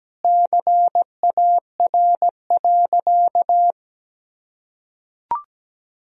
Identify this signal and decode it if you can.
Morse code, heard as a keyed tone, about 17 wpm, 700 Hz. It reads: CAR.
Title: Sentences from Top 1000 Words - Review Part 1 - 17wpm